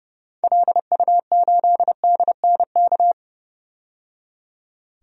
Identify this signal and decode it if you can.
Morse code, heard as a keyed tone, about 32 wpm, 700 Hz. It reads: LU8DNK
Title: DXCC-100 Call Signs - 6 Characters - 30wpm